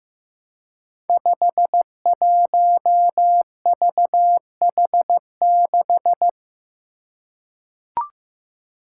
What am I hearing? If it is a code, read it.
Morse code, heard as a keyed tone, about 15 wpm, 700 Hz. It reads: 51VH6
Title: Five Challenging Letter-Number Combinations 15wpm